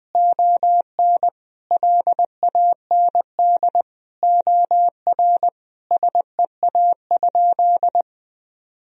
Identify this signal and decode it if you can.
Morse code, heard as a keyed tone, about 20 wpm, 700 Hz. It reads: ON LAND OR SEA?
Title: Sentences from Top 1000 Words - Review Part 1 - 20wpm